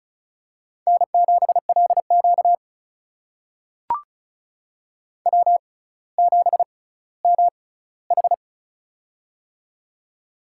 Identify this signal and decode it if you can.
Morse code, heard as a keyed tone, about 36 wpm, spaced as for 18 wpm, 700 Hz. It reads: N7LQ W7MH
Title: USA Call Signs - 4 Characters - 6x Character Spacing - 35wpm